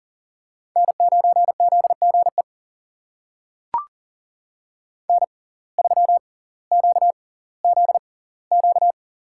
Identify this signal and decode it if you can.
Morse code, heard as a keyed tone, about 39 wpm, spaced as for 20 wpm, 700 Hz. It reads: N9ZGE N3QZQ
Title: USA Call Signs - 5 Characters - 6x Character Spacing - 40wpm